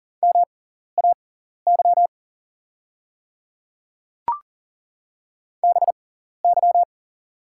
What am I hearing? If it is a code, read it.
Morse code, heard as a keyed tone, about 38 wpm, spaced as for 14 wpm, 690 Hz. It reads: MAY BY
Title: 6x Character Spacing - Top 100 Words - 40wpm